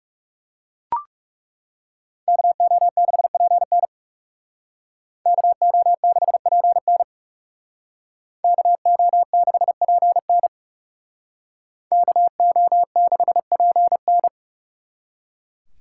Morse code, heard as a keyed tone, about 36 wpm, 700 Hz. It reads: KO6PN KO6PN KO6PN KO6PN
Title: Speed Racing - USA 2x2 Call Signs 30wpm